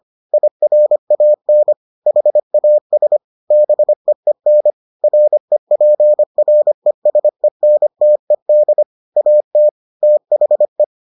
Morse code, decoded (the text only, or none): IRAN HAS BEEN REPRESENTED AT THE